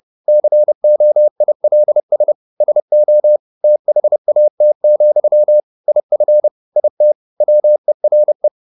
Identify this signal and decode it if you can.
Morse code, heard as a keyed tone, about 30 wpm, 600 Hz. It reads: COILS SO THAT, IF IT WERE
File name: the old man and the sea / the old man and the sea 1 30wpm